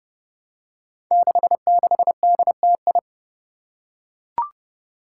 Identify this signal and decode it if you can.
Morse code, heard as a keyed tone, about 28 wpm, 700 Hz. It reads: 66DTI
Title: Five Challenging Letter-Number Combinations 30wpm